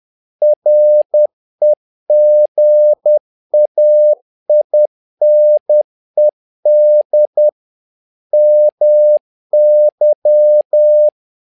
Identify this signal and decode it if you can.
Morse code, heard as a keyed tone, about 10 wpm, 600 Hz. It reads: REGAINED MY
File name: learning center / tower trouble / tower trouble 10wpm